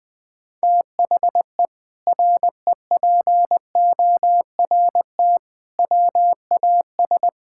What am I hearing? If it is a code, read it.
Morse code, heard as a keyed tone, about 20 wpm, 700 Hz. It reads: THE REPORT WAS
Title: Sentences from Top 1000 Words - Review Part 6 - 20wpm